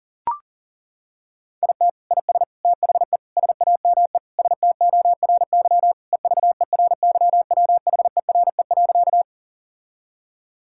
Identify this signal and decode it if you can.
Morse code, heard as a keyed tone, about 40 wpm, 700 Hz. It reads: IT IS THE SAME STORY EVERYWHERE.
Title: Sentences from Top 1600 Words 40wpm